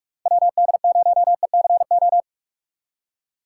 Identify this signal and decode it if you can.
Morse code, heard as a keyed tone, about 44 wpm, 700 Hz. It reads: WD0ECO
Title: K1USN SST QSO Practice - 45wpm